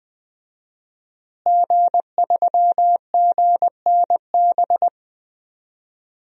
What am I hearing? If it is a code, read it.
Morse code, heard as a keyed tone, about 20 wpm, 700 Hz. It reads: G3GNB
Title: DXCC-100 Call Signs - 5 Characters - Repeated 3x - 20wpm